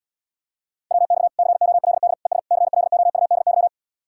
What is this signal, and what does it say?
Morse code, heard as a keyed tone, about 79 wpm, 700 Hz. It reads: WX COLD ES CLOUDY